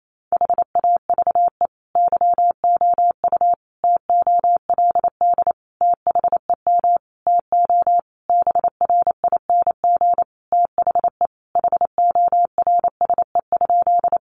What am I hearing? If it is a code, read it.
Morse code, heard as a keyed tone, about 28 wpm, 700 Hz. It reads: HAVE YOU TOLD THEM TO BRING THE HORSE?
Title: Sentences from Top 500 Words 28wpm